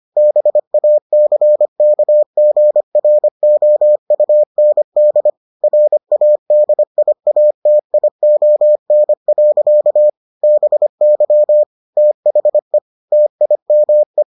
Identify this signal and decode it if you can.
Morse code, heard as a keyed tone, about 25 wpm, 600 Hz. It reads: BACKGROUND RADIATION. BY THE TIME